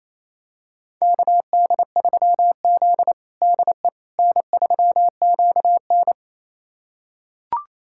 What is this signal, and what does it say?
Morse code, heard as a keyed tone, about 28 wpm, 700 Hz. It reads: KD3Z DE N3QN
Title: USA Callsign DE Callsign - 4 Characters - 28wpm